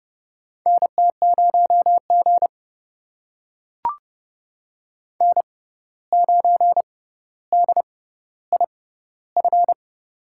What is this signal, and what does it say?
Morse code, heard as a keyed tone, about 30 wpm, spaced as for 14 wpm, 700 Hz. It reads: NT0G N9DIF
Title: USA Call Signs - 6x Character Spacing - 30wpm